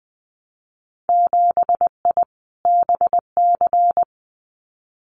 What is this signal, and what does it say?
Morse code, heard as a keyed tone, about 20 wpm, 700 Hz. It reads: 7I BC
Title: Winter Field Day Exchanges Encore 20wpm